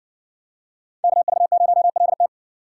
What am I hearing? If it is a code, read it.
Morse code, heard as a keyed tone, about 60 wpm, 700 Hz. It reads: KV0LT